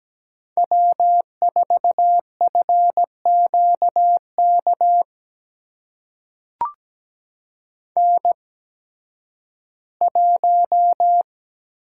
Morse code, heard as a keyed tone, about 17 wpm, spaced as for 11 wpm, 700 Hz. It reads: W4FQK N1
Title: USA Call Signs - 5 Characters - 8x Character Spacing - 17wpm